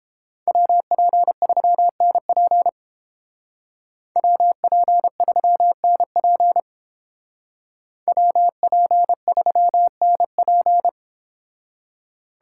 Morse code, heard as a keyed tone, about 30 wpm, 700 Hz. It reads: WP3NP WP3NP WP3NP